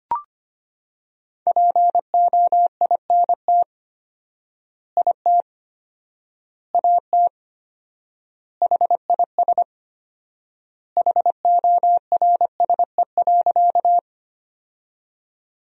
Morse code, heard as a keyed tone, about 25 wpm, 700 Hz. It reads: POINT IT AT HIS HORSE.
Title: 4x Word-Spacing - Sentences from Top 300 Words - 25wpm